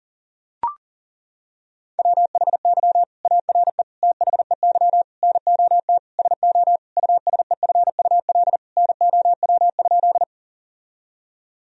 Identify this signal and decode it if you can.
Morse code, heard as a keyed tone, about 40 wpm, 700 Hz. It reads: WHY ARE THEY NOT SO USEFUL NOW?
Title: Sentences from Top 1000 Words - Review Part 4 - 40wpm